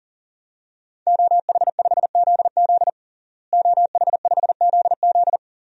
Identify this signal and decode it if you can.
Morse code, heard as a keyed tone, about 37 wpm, 700 Hz. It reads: OH5ZZ OH5ZZ
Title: DXCC-100 Call Signs - 5 Characters - Repeated 3x - 40wpm